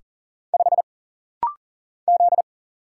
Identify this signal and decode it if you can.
Morse code, heard as a keyed tone, about 40 wpm, 700 Hz. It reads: 5 Z
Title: Single Letter-Number - Rapid-Fire 40wpm